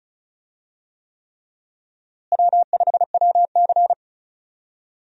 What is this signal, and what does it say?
Morse code, heard as a keyed tone, about 36 wpm, 700 Hz. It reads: W5WC
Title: USA Call Signs - 4 Characters - Repeated 3x - 35wpm